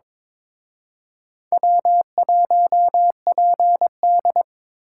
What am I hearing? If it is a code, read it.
Morse code, heard as a keyed tone, about 26 wpm, 700 Hz. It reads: W1PD
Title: Speed Racing - USA 1x2 Call Signs 22wpm